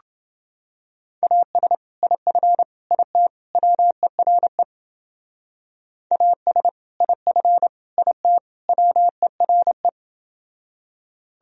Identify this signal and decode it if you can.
Morse code, heard as a keyed tone, about 29 wpm, 700 Hz. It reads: AS IF IT WERE AS IF IT WERE